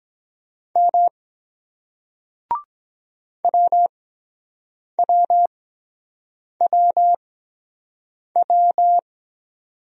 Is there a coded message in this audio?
Yes: Morse code from 0.8 to 9.0 s, heard as a keyed tone, about 22 wpm, spaced as for 7 wpm, 700 Hz. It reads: M WWWW